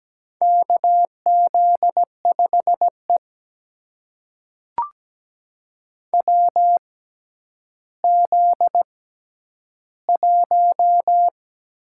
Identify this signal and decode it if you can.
Morse code, heard as a keyed tone, about 17 wpm, spaced as for 10 wpm, 700 Hz. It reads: KZ5E WZ1
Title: USA Call Signs - 4 Characters - 6x Character Spacing - 17wpm